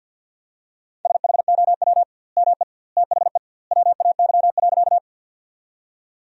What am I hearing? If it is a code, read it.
Morse code, heard as a keyed tone, about 50 wpm, 700 Hz. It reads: SHOW ME THE WAY.
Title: 2x Word-Spacing - Sentences from Top 200 Words - 50wpm